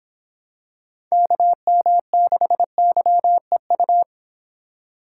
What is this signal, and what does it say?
Morse code, heard as a keyed tone, about 25 wpm, 700 Hz. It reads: KM6YEU